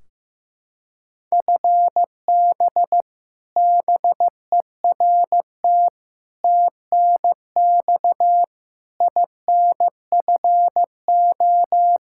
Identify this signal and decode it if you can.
Morse code, heard as a keyed tone, about 15 wpm, 700 Hz. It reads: FB BERT TNX INFO
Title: Practice QSO - Exchange 3 - More Info - 15wpm